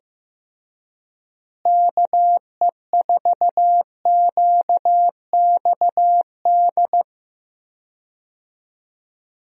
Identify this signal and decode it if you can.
Morse code, heard as a keyed tone, about 15 wpm, 700 Hz. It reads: KE4QXD
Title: USA Call Signs - 6 Characters - Repeated 3x - 15wpm